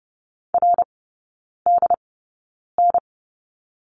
Morse code, heard as a keyed tone, about 30 wpm, spaced as for 9 wpm, 700 Hz. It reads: RDN
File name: course / 30:10wpm / 087 Lesson 15e - Number 5 - Three Characters 30wpm